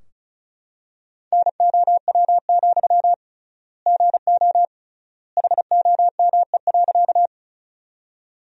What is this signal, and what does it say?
Morse code, heard as a keyed tone, about 36 wpm, 700 Hz. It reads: NOW, GO HOME.
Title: 3x Word-Spacing - Sentences from Top 200 Words - 35wpm